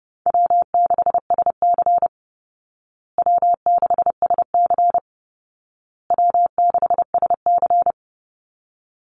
Morse code, heard as a keyed tone, about 30 wpm, 700 Hz. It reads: W6SC W6SC W6SC
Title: Call Signs Repeated 2x - 30wpm